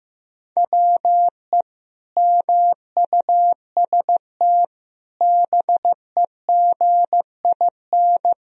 Morse code, heard as a keyed tone, about 15 wpm, 700 Hz. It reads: WE MUST BEGIN
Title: Sentences from Top 1000 Words - Review Part 1 - 15wpm